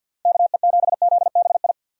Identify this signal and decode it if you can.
Morse code, heard as a keyed tone, about 52 wpm, 700 Hz. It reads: KE7ZBI
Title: USA Call Signs - 6 Characters - Repeated 2x - 50wpm